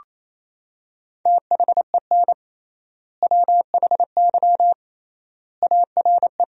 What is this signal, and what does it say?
Morse code, heard as a keyed tone, about 28 wpm, 700 Hz. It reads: THEN WHY ARE
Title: 3x Word-Spacing - Sentences from Top 200 Words - 28wpm